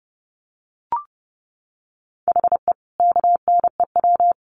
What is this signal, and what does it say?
Morse code, heard as a keyed tone, about 30 wpm, 700 Hz. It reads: HE KNEW
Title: Sentences from Top 900 Words 30wpm